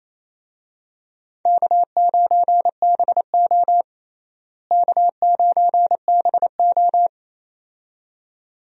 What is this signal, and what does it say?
Morse code, heard as a keyed tone, about 28 wpm, 700 Hz. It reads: K9BO K9BO
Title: USA Call Signs - 4 Characters - Repeated 2x - 28wpm